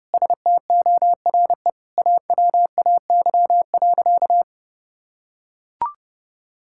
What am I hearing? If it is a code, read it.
Morse code, heard as a keyed tone, about 32 wpm, 700 Hz. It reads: STORE AWAY.
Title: Sentences from Top 1000 Words - Review Part 5 - 30wpm